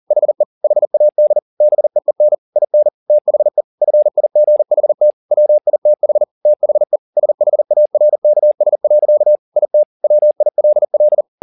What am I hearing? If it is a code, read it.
Morse code, heard as a keyed tone, about 40 wpm, 600 Hz. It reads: HE HAD BEEN IN THE FIGHT WITH THE SHARKS. IT WILL
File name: the old man and the sea 2 40wpm